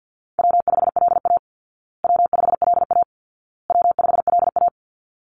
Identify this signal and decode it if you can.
Morse code, heard as a keyed tone, about 49 wpm, 700 Hz. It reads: W5LA W5LA W5LA
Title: Call Signs Repeated 2x - 50wpm